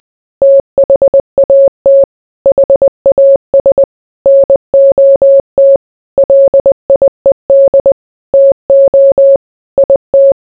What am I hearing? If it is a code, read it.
Morse code, heard as a keyed tone, about 20 wpm, 550 Hz. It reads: THAT HAS NOT LIED TO IT